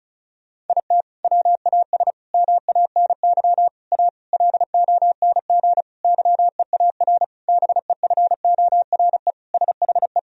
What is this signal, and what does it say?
Morse code, heard as a keyed tone, about 34 wpm, 700 Hz. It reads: IT WAS MANY A LONG YEAR BEFORE SHE
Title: Sentences from Top 1000 Words - Review Part 1 - 35wpm